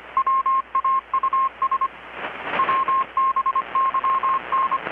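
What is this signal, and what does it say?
Morse code, heard as a keyed tone, about 25 wpm, 1070 Hz, with radio noise. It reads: WAUS WBYN
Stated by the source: Morse from KPH's Night of Nights XX maritime radiotelegraphy event, received in Chiapas, Mexico about 3,800 km away on a Kenwood R-600 with a grounded 10 m long-wire antenna.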